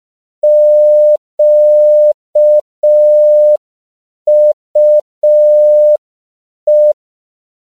Morse code, heard as a keyed tone, about 5 wpm, 600 Hz. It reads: QUE
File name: CW / 5PPM